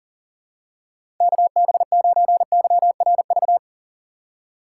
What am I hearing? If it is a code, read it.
Morse code, heard as a keyed tone, about 38 wpm, 700 Hz. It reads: KB9YRV
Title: USA Call Signs - 3x Character Spacing - 40wpm